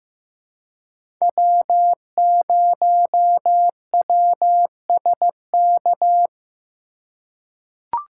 Morse code, heard as a keyed tone, about 15 wpm, 700 Hz. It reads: W0WSK
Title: USA Call Signs - 5 Characters - 3x Character Spacing - 15wpm